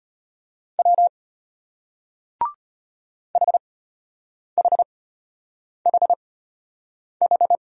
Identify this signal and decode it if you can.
Morse code, heard as a keyed tone, about 34 wpm, spaced as for 8 wpm, 700 Hz. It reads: W HHHH